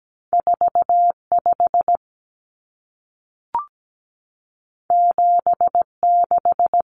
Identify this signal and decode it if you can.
Morse code, heard as a keyed tone, about 17 wpm, 700 Hz. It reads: 45 76
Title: Numbers - 2-Digit 17wpm